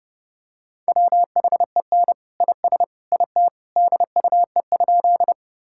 Morse code, heard as a keyed tone, about 30 wpm, 700 Hz. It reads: WHEN IS IT DUE?
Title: Sentences from Top 1000 Words - Review Part 6 - 30wpm